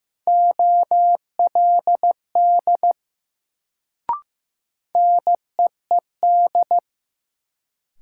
Morse code, heard as a keyed tone, about 15 wpm, 700 Hz. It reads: OLD NEED
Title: Sentences from Top 200 Words - New Word Drill - 15wpm